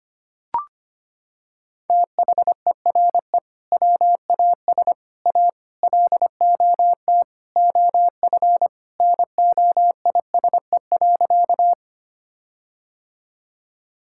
Morse code, heard as a keyed tone, about 25 wpm, 700 Hz. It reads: THERE WAS A LOT OF NOISE.
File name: practice / 25wpm / Sentences with 4-Word Phrases - 25wpm